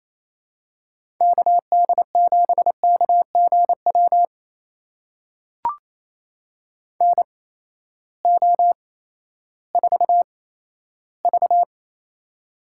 Morse code, heard as a keyed tone, about 28 wpm, spaced as for 14 wpm, 700 Hz. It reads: KD7KGW NO4V
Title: USA Call Signs - 8x Character Spacing - 28wpm